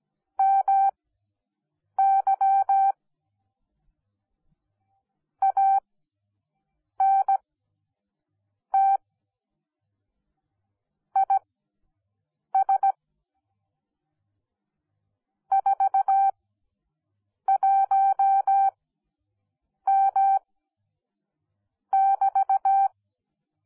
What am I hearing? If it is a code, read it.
Morse code, heard as a keyed tone, about 17 wpm, spaced as for 6 wpm, 780 Hz. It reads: MY ANT IS 41M<BT>
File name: Onlinelehrgang / morsen / DJ4UF Morsen10